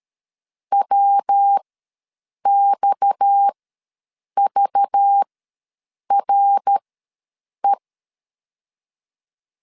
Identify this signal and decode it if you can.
Morse code, heard as a keyed tone, about 13 wpm, spaced as for 7 wpm, 800 Hz. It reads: WXVRE
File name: LEZIONI / 6 PKX0 / 6 PKX ws 12wpm 3106